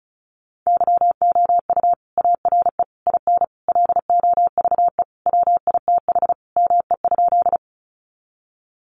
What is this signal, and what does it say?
Morse code, heard as a keyed tone, about 34 wpm, 700 Hz. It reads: YOU ARE IN LOVE WITH ME?
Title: Sentences from Top 400 Words 35wpm